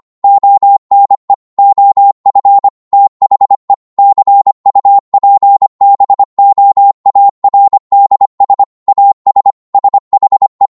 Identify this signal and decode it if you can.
Morse code, heard as a keyed tone, about 25 wpm, 800 Hz. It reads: ONE OF THE CUPBOARDS AS SHE